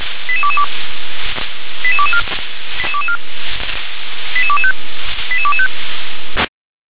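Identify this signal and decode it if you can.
Five-tone selective call (ZVEI), heard as quick runs of five tones, tones of about 70 ms. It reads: 88222 88224 88225 88225
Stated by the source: ZVEI five-tone selective call received by SDR.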